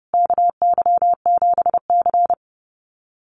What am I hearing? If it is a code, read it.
Morse code, heard as a keyed tone, about 30 wpm, 700 Hz. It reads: KY7C
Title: USA Call Signs Repeated 3x 30wpm